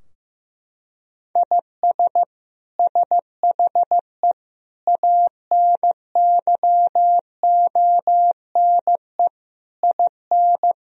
Morse code, heard as a keyed tone, about 15 wpm, 700 Hz. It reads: IS SHE ANYONE IN
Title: Sentences from Top 1000 Words - Review Part 4 - 15wpm